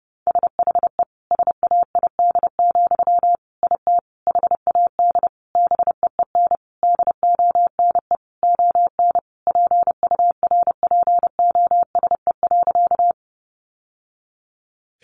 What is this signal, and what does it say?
Morse code, heard as a keyed tone, about 30 wpm, 700 Hz. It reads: SHE SAID, IT HAD BEEN DONE ON PURPOSE.
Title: Sentences from Top 800 Words 30wpm